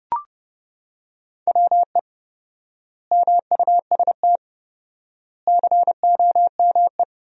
Morse code, heard as a keyed tone, about 32 wpm, 700 Hz. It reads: WE MUST COME
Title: 4x Word-Spacing - Sentences from Top 200 Words - 30wpm